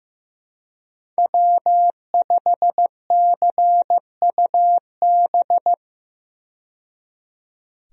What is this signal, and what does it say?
Morse code, heard as a keyed tone, about 15 wpm, 700 Hz. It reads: W5CUB
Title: USA Call Signs - 5 Characters - 15wpm